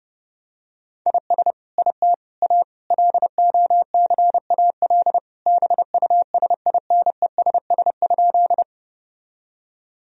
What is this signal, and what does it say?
Morse code, heard as a keyed tone, about 30 wpm, 700 Hz. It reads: IS IT A LOCAL BUSINESS?
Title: Sentences from Top 1000 Words - Review Part 3 - 30wpm